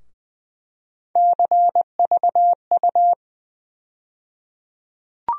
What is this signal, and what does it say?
Morse code, heard as a keyed tone, about 20 wpm, 700 Hz. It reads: CVU
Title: USA Call Sign Postfixes - 3 Letters 20wpm